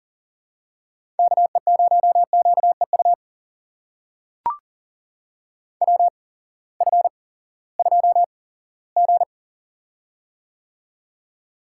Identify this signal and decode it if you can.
Morse code, heard as a keyed tone, about 39 wpm, spaced as for 20 wpm, 700 Hz. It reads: KE0QEU WF2G
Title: USA Call Signs - 8x Character Spacing - 40wpm